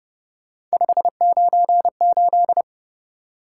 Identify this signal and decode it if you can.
Morse code, heard as a keyed tone, about 31 wpm, 700 Hz. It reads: 598